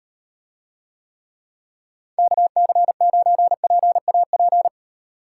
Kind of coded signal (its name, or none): Morse code